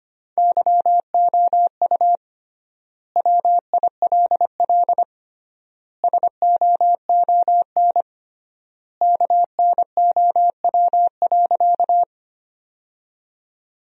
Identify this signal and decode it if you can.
Morse code, heard as a keyed tone, about 25 wpm, 700 Hz. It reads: YOU WILL SOON KNOW.
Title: 3x Word-Spacing - Sentences from Top 400 Words - 25wpm